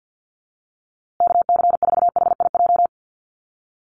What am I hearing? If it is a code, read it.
Morse code, heard as a keyed tone, about 50 wpm, 700 Hz. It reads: KC4HIJ
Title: USA Call Signs Repeated 2x 50wpm